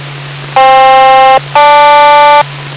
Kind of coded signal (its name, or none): aircraft SELCAL